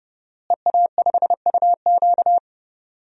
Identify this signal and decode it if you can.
Morse code, heard as a keyed tone, about 30 wpm, 700 Hz. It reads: EA5UQ
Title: DXCC-100 Call Signs - 5 Characters - Repeated 2x - 30wpm